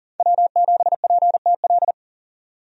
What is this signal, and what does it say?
Morse code, heard as a keyed tone, about 40 wpm, 700 Hz. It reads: W7PTL
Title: USA Call Signs - 5 Characters - 40wpm